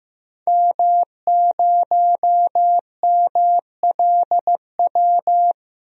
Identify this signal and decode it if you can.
Morse code, heard as a keyed tone, about 15 wpm, 700 Hz. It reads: M0MLW
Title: DXCC-100 Call Signs - 5 Characters - 15wpm